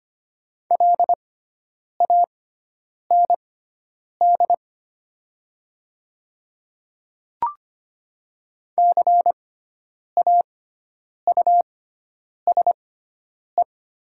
Morse code, heard as a keyed tone, about 25 wpm, spaced as for 9 wpm, 700 Hz. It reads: LAND CAUSE
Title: 6x Character Spacing - Top 200 Words - 25wpm